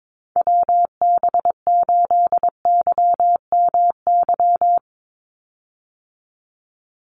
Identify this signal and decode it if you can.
Morse code, heard as a keyed tone, about 22 wpm, 700 Hz. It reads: WB8YMY
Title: USA Call Signs 22wpm